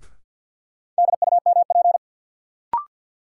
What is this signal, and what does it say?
Morse code, heard as a keyed tone, about 52 wpm, 700 Hz. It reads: DUMP
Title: Sentences from Top 1300 Words - New Word Drill - 50wpm